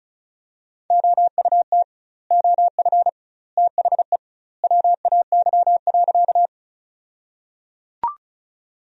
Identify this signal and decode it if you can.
Morse code, heard as a keyed tone, about 34 wpm, 700 Hz. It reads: OUT OF THE WAY.
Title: 2x Word-Spacing - Sentences from Top 100 Words - 35wpm